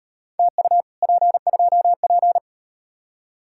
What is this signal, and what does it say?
Morse code, heard as a keyed tone, about 38 wpm, 700 Hz. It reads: TU P2P